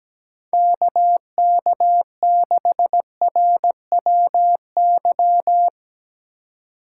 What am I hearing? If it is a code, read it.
Morse code, heard as a keyed tone, about 19 wpm, 700 Hz. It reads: KK6RWY
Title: Speed Racing - USA Call Signs 17wpm